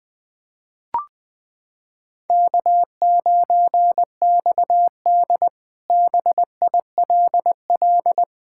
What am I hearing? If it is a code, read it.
Morse code, heard as a keyed tone, about 20 wpm, 700 Hz. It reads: K9XD BILL
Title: K1USN SST QSO Practice - 20wpm